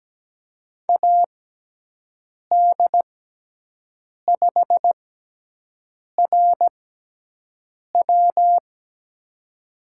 Morse code, heard as a keyed tone, about 17 wpm, spaced as for 6 wpm, 700 Hz. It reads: AD5RW
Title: USA Call Signs - 6x Character Spacing - 17wpm